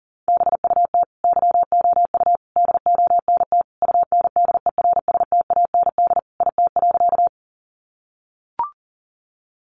Morse code, heard as a keyed tone, about 40 wpm, 700 Hz. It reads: BUT YOU DONT UNDERSTAND IT.
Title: Sentences from Top 600 Words 40wpm